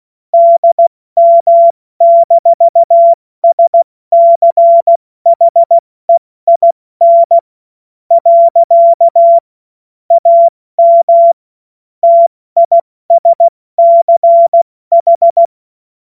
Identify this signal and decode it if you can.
Morse code, heard as a keyed tone, about 16 wpm, spaced as for 13 wpm, 680 Hz. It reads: DM-SCHEIN. AM TISCH
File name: Witz-21-Oktober-2025-70-F80